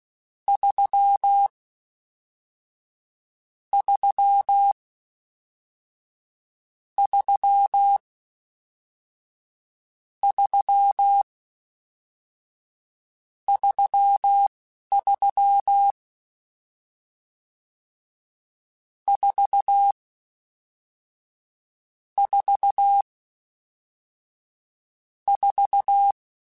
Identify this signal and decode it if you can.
Morse code, heard as a keyed tone, about 16 wpm, spaced as for 11 wpm, 800 Hz. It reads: 3 3 3 3 33 4 4 4